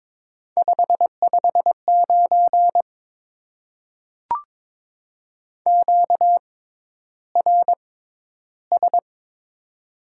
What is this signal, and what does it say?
Morse code, heard as a keyed tone, about 22 wpm, spaced as for 11 wpm, 700 Hz. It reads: 559 QRS
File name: Top 90 QSO Elements - 6x Character Spacing - 22wpm